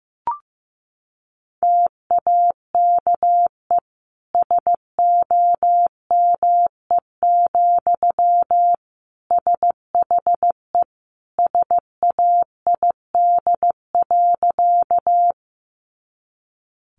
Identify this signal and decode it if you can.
Morse code, heard as a keyed tone, about 15 wpm, 700 Hz. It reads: TAKE SOME, SHE SAID.
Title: Sentences from Top 200 Words 15wpm